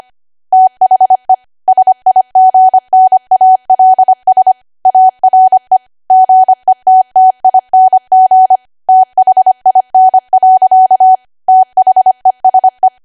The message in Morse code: THE SIGNALS ARE GETTING THIN. THESE